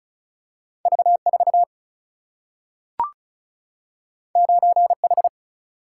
Morse code, heard as a keyed tone, about 36 wpm, 700 Hz. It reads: V 4 9 H
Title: Single-Letter Single-Number 35wpm